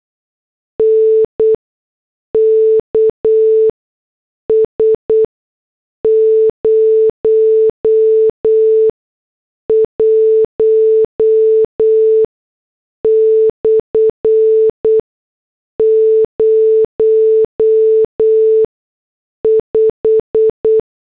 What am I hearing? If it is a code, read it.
Morse code, heard as a keyed tone, about 8 wpm, 440 Hz. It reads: N K S 0 1 / 0 5
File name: morse / headlines